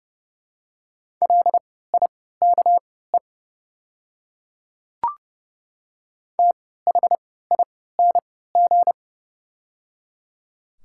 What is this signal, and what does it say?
Morse code, heard as a keyed tone, about 30 wpm, spaced as for 17 wpm, 700 Hz. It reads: LIKE THING
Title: 3x Character Spacing - Top 100 Words - 30wpm